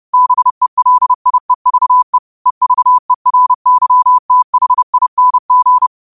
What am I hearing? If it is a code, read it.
Morse code, heard as a keyed tone, about 30 wpm, 1000 Hz. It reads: BELIEVE EVERYTHING
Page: New quotes every day in morse code at 30 Words per minute.